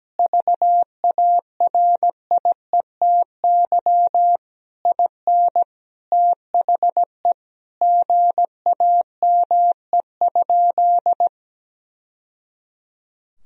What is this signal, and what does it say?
Morse code, heard as a keyed tone, about 17 wpm, 700 Hz. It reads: VARIETY IN THE GAME?
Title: Sentences from Top 1000 Words - Review Part 4 - 17wpm